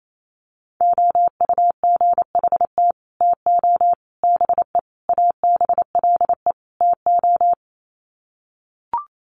Morse code, heard as a keyed tone, about 28 wpm, 700 Hz. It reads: OUGHT TO BE ABLE TO
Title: Sets of 5 Words Encore2 28wpm